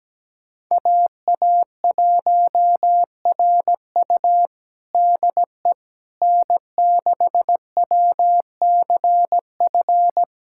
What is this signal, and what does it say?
Morse code, heard as a keyed tone, about 17 wpm, 700 Hz. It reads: AA1RU DE N6WCF
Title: USA Callsign DE Callsign - 5 Characters - 17wpm